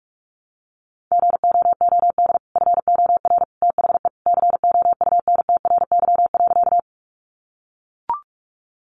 Morse code, heard as a keyed tone, about 44 wpm, 700 Hz. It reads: GOOD FOR THE COUNTRY.
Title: Sentences from Top 300 Words 45wpm